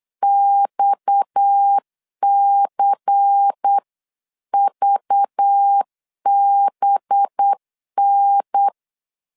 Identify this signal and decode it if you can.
Morse code, heard as a keyed tone, about 8 wpm, 800 Hz. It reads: XCVBN
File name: 8wpm